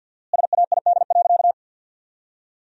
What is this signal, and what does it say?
Morse code, heard as a keyed tone, about 48 wpm, 700 Hz. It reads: SAID.
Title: 4x Word-Spacing - Sentences from Top 200 Words - 50wpm